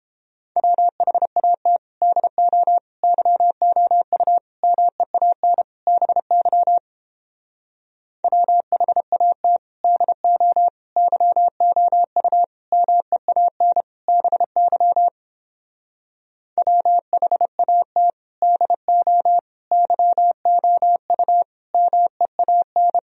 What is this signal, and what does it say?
Morse code, heard as a keyed tone, about 29 wpm, 700 Hz. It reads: WHAT DO YOU MEAN BY WHAT DO YOU MEAN BY WHAT DO YOU MEAN